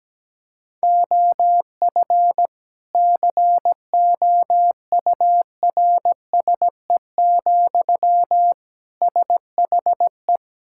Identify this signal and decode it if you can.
Morse code, heard as a keyed tone, about 17 wpm, 700 Hz. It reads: OF COURSE, SHE
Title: Sentences from Top 1600 Words 17wpm